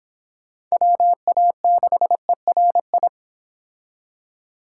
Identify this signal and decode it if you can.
Morse code, heard as a keyed tone, about 26 wpm, 700 Hz. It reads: WA6ERI